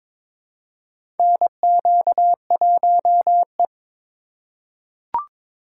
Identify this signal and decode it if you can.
Morse code, heard as a keyed tone, about 22 wpm, 700 Hz. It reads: NQ1E
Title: USA Call Signs - 4 Characters - 4x Character Spacing - 22wpm